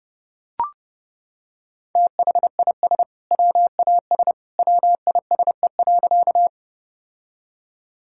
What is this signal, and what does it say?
Morse code, heard as a keyed tone, about 30 wpm, 700 Hz. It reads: THIS WAS WISE.
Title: Sentences from Top 1600 Words 30wpm